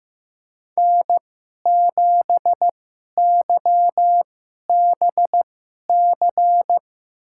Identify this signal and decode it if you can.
Morse code, heard as a keyed tone, about 15 wpm, spaced as for 11 wpm, 700 Hz. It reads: N7YBC
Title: USA Call Signs - 5 Characters - 2x Character Spacing - 15wpm